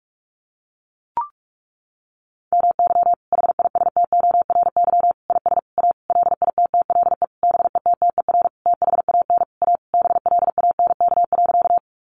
Morse code, heard as a keyed tone, about 44 wpm, 700 Hz. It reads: MY HISTORY IS A LITTLE BETTER THAN A BLANK.
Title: Sentences from Top 1500 Words 45wpm